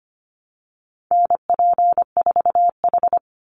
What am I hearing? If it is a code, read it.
Morse code, heard as a keyed tone, about 25 wpm, 700 Hz. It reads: NP4H
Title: Call Signs Repeated 2x - 25wpm